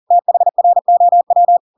Morse code, heard as a keyed tone, about 40 wpm, 700 Hz. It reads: THROW
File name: Pinocchio / Pinocchio ch7 40 40 ik1hge